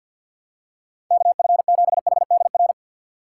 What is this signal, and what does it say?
Morse code, heard as a keyed tone, about 47 wpm, 700 Hz. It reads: KF7HDR
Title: USA Call Signs - 6 Characters - 8x Character Spacing - 50wpm